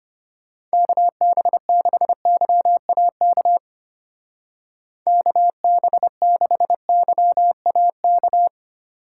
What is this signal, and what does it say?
Morse code, heard as a keyed tone, about 31 wpm, 700 Hz. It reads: KB6YAK KB6YAK